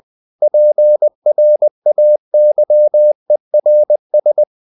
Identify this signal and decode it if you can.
Morse code, heard as a keyed tone, about 20 wpm, 600 Hz. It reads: PRAYERS
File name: the old man and the sea / the old man and the sea part 1 20wpm